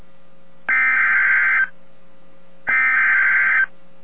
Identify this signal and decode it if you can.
Emergency Alert System (SAME) tones (header bursts), heard as harsh buzzing bursts. It is heard from 0.7 to 3.6 s.